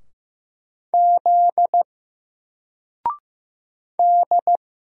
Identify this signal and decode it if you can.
Morse code, heard as a keyed tone, about 15 wpm, 700 Hz. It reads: Z D
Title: Single Character (Letter:Number:Punctuation:Prosign) 15wpm